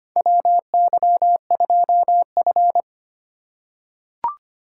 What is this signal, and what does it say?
Morse code, heard as a keyed tone, about 25 wpm, 700 Hz. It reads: WY2F
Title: USA Call Signs - 4 Characters - 8x Character Spacing - 25wpm